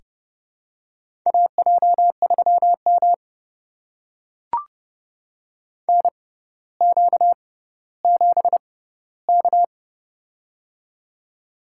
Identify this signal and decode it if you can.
Morse code, heard as a keyed tone, about 31 wpm, spaced as for 15 wpm, 700 Hz. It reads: AJ3M NQ7K
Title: USA Call Signs - 4 Characters - 6x Character Spacing - 30wpm